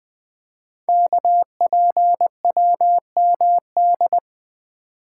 Morse code, heard as a keyed tone, about 20 wpm, 700 Hz. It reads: KPWMD
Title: Code Groups - 5 Letters - 20wpm